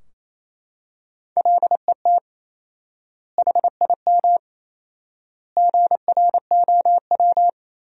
Morse code, heard as a keyed tone, about 28 wpm, 700 Hz. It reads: LET HIM GROW
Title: 4x Word-Spacing - Sentences from Top 300 Words - 28wpm